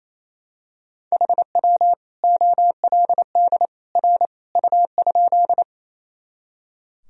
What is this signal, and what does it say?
Morse code, heard as a keyed tone, about 28 wpm, 700 Hz. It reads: HW OLD R U?
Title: Sentences from Top 200 Words and Abbreviations 28wpm